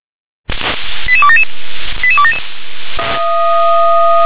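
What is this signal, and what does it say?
Five-tone selective call (ZVEI), heard as quick runs of five tones, tones of about 70 ms. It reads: 88277 88277
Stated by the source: ZVEI five-tone selective call received by SDR.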